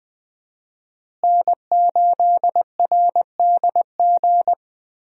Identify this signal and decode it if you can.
Morse code, heard as a keyed tone, about 20 wpm, 700 Hz. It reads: N8RDG
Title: USA Call Signs - 5 Characters - Repeated 2x - 20wpm